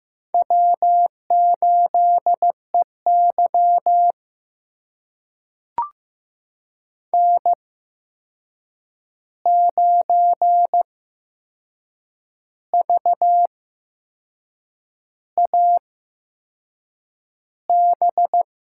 Morse code, heard as a keyed tone, about 15 wpm, spaced as for 6 wpm, 700 Hz. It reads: W8EY N9VAB
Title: USA Call Signs - 8x Character Spacing - 15wpm